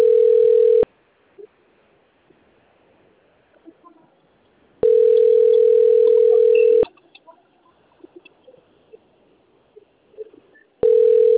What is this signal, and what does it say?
Ringback tone (North America), 440 + 480 Hz.